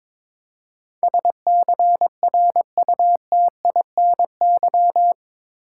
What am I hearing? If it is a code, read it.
Morse code, heard as a keyed tone, about 21 wpm, 700 Hz. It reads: SCRUTINY
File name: practice / 22wpm / Eight Letter Words - 22wpm